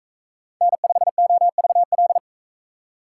Morse code, heard as a keyed tone, about 41 wpm, 700 Hz. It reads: N5OVL